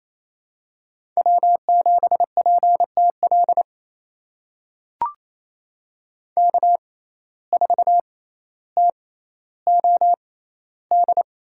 Morse code, heard as a keyed tone, about 29 wpm, spaced as for 14 wpm, 700 Hz. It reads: W7PTL K4TOD
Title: USA Call Signs - 5 Characters - 6x Character Spacing - 28wpm